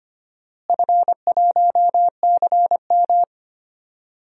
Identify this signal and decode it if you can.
Morse code, heard as a keyed tone, about 25 wpm, 700 Hz. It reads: F1CM
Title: DXCC-100 Call Signs - 4 Characters - Repeated 2x - 25wpm